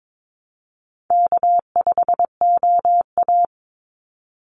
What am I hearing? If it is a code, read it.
Morse code, heard as a keyed tone, about 22 wpm, 700 Hz. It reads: K5OA
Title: Easy Call Signs Repeated 2x - 22wpm